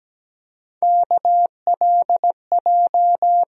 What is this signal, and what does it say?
Morse code, heard as a keyed tone, about 17 wpm, 700 Hz. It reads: KLJ
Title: USA Call Sign Postfixes - 3 Letters 17wpm